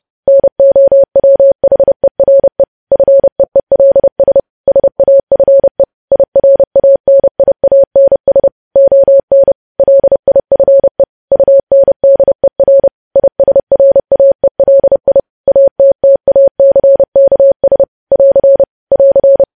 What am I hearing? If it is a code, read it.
Morse code, heard as a keyed tone, about 30 wpm, 550 Hz. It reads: NOWHERE FEELS SAFE IRANIANS ON LIFE UNDER ISRAELI ATTACKS <AR> <AR>
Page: news headlines in morse code at 30 wpm